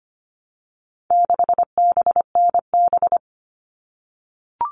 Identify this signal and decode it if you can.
Morse code, heard as a keyed tone, about 25 wpm, 700 Hz. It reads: 6BNB
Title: All Cut Numbers - 4-Digit 25wpm